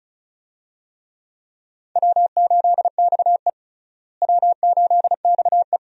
Morse code, heard as a keyed tone, about 34 wpm, 700 Hz. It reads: W8XE W8XE
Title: USA Call Signs - 4 Characters - Repeated 2x - 35wpm